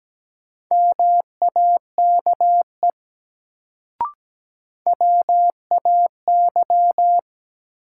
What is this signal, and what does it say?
Morse code, heard as a keyed tone, about 17 wpm, 700 Hz. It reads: MAKE WAY
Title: Sentences from Top 100 Words - New Word Drill - 17wpm